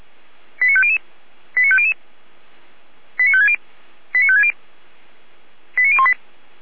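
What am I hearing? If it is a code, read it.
Five-tone selective call (ZVEI), heard as quick runs of five tones, tones of about 80 ms. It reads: 89500 89500 89560 89560 89018
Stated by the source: ZVEI five-tone call 89350 received on an SDR.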